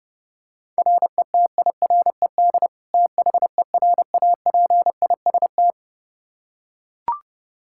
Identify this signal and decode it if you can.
Morse code, heard as a keyed tone, about 30 wpm, 700 Hz. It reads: RETIRED THERAPIST
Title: Practice QSO - Exchange 3 - More Info - 30wpm